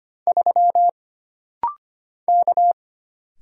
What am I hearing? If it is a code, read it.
Morse code, heard as a keyed tone, about 25 wpm, 700 Hz. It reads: 3 K
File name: Single Character (Letter:Number:Punctuation:Prosign) - Rapid-Fire - 25wpm